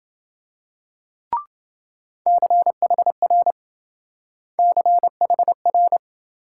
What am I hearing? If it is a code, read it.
Morse code, heard as a keyed tone, about 26 wpm, 700 Hz. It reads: CHR CHR